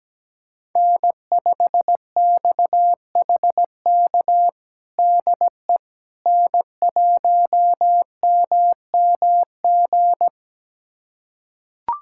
Morse code, heard as a keyed tone, about 17 wpm, 700 Hz. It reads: N5XHK DE N1MMG
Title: USA Callsign DE Callsign - 5 Characters - 17wpm